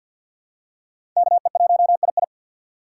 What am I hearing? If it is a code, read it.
Morse code, heard as a keyed tone, about 53 wpm, 700 Hz. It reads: KE1II
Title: USA Call Signs - 5 Characters - 3x Character Spacing - 50wpm